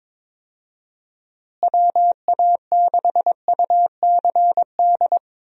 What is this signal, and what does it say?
Morse code, heard as a keyed tone, about 22 wpm, 700 Hz. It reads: WA6UCD
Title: USA Call Signs - 6 Characters - Repeated 3x - 22wpm